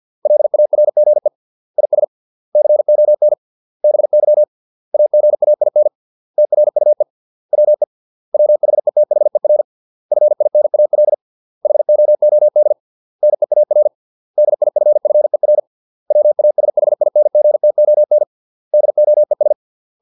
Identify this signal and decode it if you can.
Morse code, heard as a keyed tone, about 50 wpm, 600 Hz. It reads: LARGE IS CON BY AGAIN TRUE WE WHETHER FINAL HOOD NEAR DIFFER WASHINGTON DOES
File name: 632 words / 632 Words 50WPM